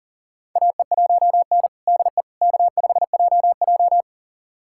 Morse code, heard as a keyed tone, about 40 wpm, 700 Hz. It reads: AE1N DE K5JJ
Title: USA Callsign DE Callsign - 4 Characters - 40wpm